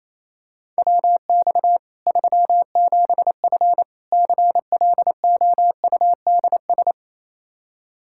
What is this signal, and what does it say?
Morse code, heard as a keyed tone, about 28 wpm, 700 Hz. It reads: WX 37F CLOUDS
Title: Practice QSO - Exchange 2 - Info - 28wpm